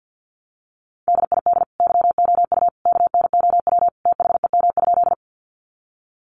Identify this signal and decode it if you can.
Morse code, heard as a keyed tone, about 50 wpm, 700 Hz. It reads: DID YOU KNOW THEM?
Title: Sentences from Top 100 Words 50wpm